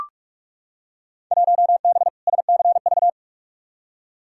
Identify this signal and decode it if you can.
Morse code, heard as a keyed tone, about 44 wpm, 700 Hz. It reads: 1B SCV
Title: Field Day Exchanges - 45wpm